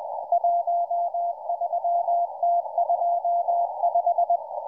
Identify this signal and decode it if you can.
Morse code, heard as a keyed tone, about 20 wpm, 690 Hz, with radio noise. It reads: 13T25